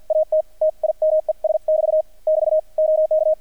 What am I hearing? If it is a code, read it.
Morse code, heard as a keyed tone, about 51 wpm, spaced as for 44 wpm, 630 Hz, with radio noise. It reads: AT TIMES <BT> <BT> OK
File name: archived / CRSnet